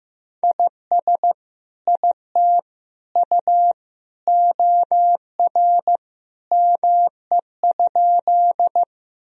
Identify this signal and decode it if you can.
Morse code, heard as a keyed tone, about 16 wpm, 700 Hz. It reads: IS IT U OR ME?
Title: Sentences from Top 200 Words and Abbreviations 15wpm